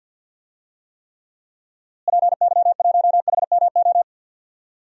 Morse code, heard as a keyed tone, about 50 wpm, 700 Hz. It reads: PY1HMO
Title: DXCC-100 Call Signs - 6 Characters - Repeated 2x - 50wpm